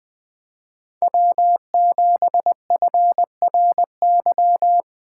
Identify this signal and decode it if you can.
Morse code, heard as a keyed tone, about 20 wpm, 700 Hz. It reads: W7FRY